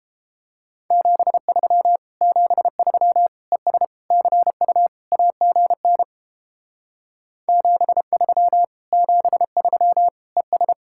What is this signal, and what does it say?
Morse code, heard as a keyed tone, about 31 wpm, 700 Hz. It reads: 73 73 ES CU AGN 73 73 ES